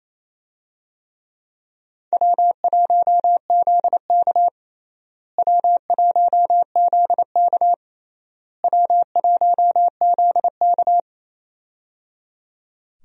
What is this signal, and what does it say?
Morse code, heard as a keyed tone, about 28 wpm, 700 Hz. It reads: W1ZK W1ZK W1ZK
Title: USA Call Signs - 4 Characters - Repeated 3x - 28wpm